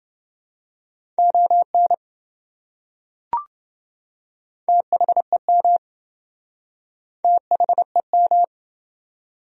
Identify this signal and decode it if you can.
Morse code, heard as a keyed tone, about 27 wpm, 700 Hz. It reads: ON THEM THEM